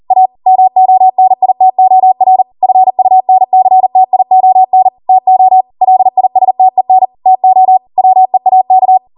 Morse code, heard as a keyed tone, about 40 wpm, 750 Hz. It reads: A MONITOR FUNCTION TO LISTEN TO WEAK
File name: Morse / 40WPM